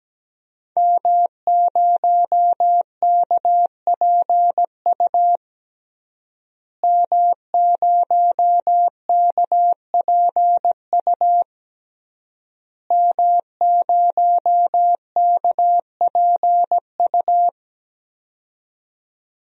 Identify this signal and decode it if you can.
Morse code, heard as a keyed tone, about 17 wpm, 700 Hz. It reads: M0KPU M0KPU M0KPU
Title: DXCC-100 Call Signs - 5 Characters - Repeated 3x - 17wpm